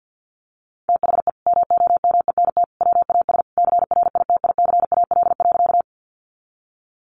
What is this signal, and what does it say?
Morse code, heard as a keyed tone, about 50 wpm, 700 Hz. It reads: THE MOMENT WAS CRITICAL.
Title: Sentences from Top 1100 Words 50wpm